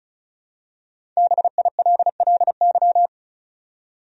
Morse code, heard as a keyed tone, about 34 wpm, 700 Hz. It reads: BILLY
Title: USA Common Names - 35wpm